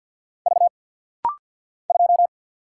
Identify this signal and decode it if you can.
Morse code, heard as a keyed tone, about 49 wpm, 700 Hz. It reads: V 2
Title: Single Letter-Number - Rapid-Fire 50wpm